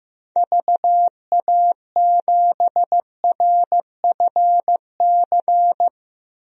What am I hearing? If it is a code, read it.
Morse code, heard as a keyed tone, about 15 wpm, 700 Hz. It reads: VA7RFC
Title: DXCC-100 Call Signs - 6 Characters - Repeated 3x - 15wpm